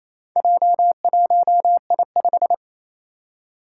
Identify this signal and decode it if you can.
Morse code, heard as a keyed tone, about 28 wpm, 700 Hz. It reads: J1I5
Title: Four Challenging Letter-Number Combinations 28wpm